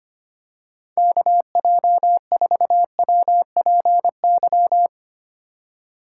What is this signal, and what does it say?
Morse code, heard as a keyed tone, about 25 wpm, 700 Hz. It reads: KJ4WPY
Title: USA Call Signs - 6 Characters - 3x Character Spacing - 25wpm